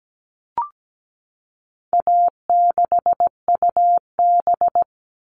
Morse code, heard as a keyed tone, about 17 wpm, 700 Hz. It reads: A6UB
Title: All Cut Numbers - 4-Digit 17wpm